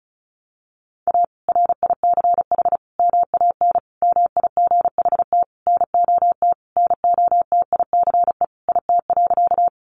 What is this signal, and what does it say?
Morse code, heard as a keyed tone, about 34 wpm, 700 Hz. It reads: A RICH MAN MIGHT NOT NOTICE IT.
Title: Sentences from Top 1100 Words 35wpm